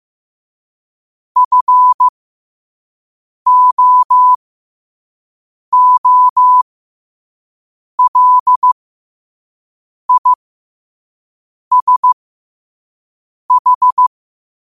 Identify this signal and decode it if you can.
Morse code, heard as a keyed tone, about 15 wpm, spaced as for 5 wpm, 1000 Hz. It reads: FOOLISH
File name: 5wpm